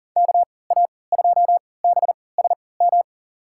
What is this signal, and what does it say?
Morse code, heard as a keyed tone, about 40 wpm, spaced as for 23 wpm, 700 Hz. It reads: KA2BSM
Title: USA Call Signs - 6 Characters - 3x Character Spacing - 40wpm